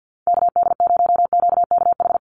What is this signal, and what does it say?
Morse code, heard as a keyed tone, about 50 wpm, 700 Hz. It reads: KD0QKH
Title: USA Call Signs 50wpm